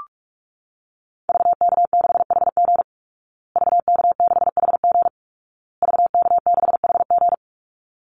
Morse code, heard as a keyed tone, about 44 wpm, 700 Hz. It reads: VK6HG VK6HG VK6HG
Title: Call Signs Repeated 2x - 45wpm